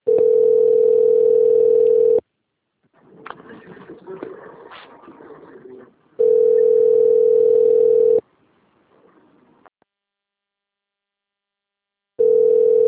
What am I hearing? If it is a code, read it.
Ringback tone (North America), 440 + 480 Hz.